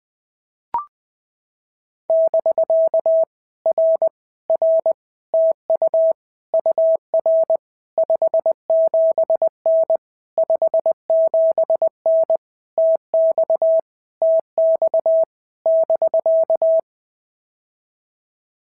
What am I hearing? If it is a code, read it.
Morse code, heard as a keyed tone, about 20 wpm, 650 Hz. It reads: <BK> R R TU UR 57N 57N TX TX <BK>
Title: POTA QSO Practice - 20wpm